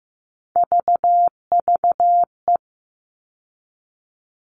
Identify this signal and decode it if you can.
Morse code, heard as a keyed tone, about 15 wpm, 700 Hz. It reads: VVE
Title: All Cut Numbers - 3-Digit 15wpm